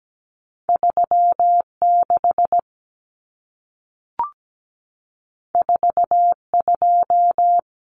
Morse code, heard as a keyed tone, about 17 wpm, 700 Hz. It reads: 36 42
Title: Common Cut Numbers - 2-Digit 17wpm